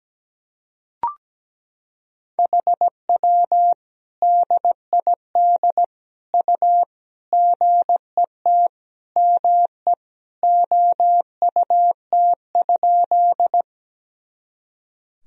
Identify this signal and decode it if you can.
Morse code, heard as a keyed tone, about 17 wpm, 700 Hz. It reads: HW DID U GET ME OUT?
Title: Sentences from Top 200 Words and Abbreviations 17wpm